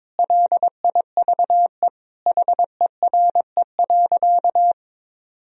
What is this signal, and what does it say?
Morse code, heard as a keyed tone, about 22 wpm, 700 Hz. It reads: LIVE HERE.
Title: Sentences from Top 1000 Words - Review Part 1 - 22wpm